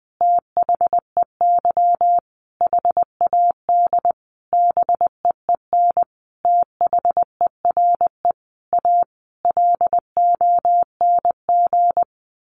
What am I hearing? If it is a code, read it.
Morse code, heard as a keyed tone, about 20 wpm, 700 Hz. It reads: THEY HAD BEEN THERE A LONG
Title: Sentences from Top 400 Words 20wpm